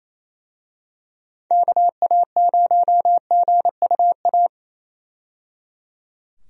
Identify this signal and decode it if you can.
Morse code, heard as a keyed tone, about 28 wpm, 700 Hz. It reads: KA0GUA
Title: USA Call Signs - 6 Characters - 28wpm